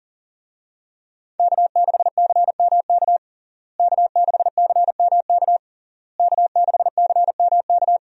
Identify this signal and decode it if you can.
Morse code, heard as a keyed tone, about 40 wpm, 700 Hz. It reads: K6CMK K6CMK K6CMK
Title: USA Call Signs - 5 Characters - Repeated 3x - 40wpm